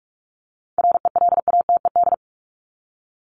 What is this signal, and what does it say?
Morse code, heard as a keyed tone, about 44 wpm, 700 Hz. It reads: RELATED
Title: ED Suffix - Top 100 Words with Base Words 45wpm